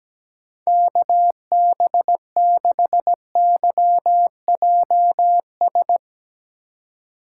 Morse code, heard as a keyed tone, about 17 wpm, 700 Hz. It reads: KB6YJS
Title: USA Call Signs - 4x Character Spacing - 17wpm